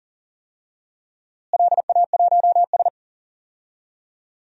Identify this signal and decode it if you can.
Morse code, heard as a keyed tone, about 37 wpm, 700 Hz. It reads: LA1S
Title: DXCC-100 Call Signs - 4 Characters - Repeated 2x - 40wpm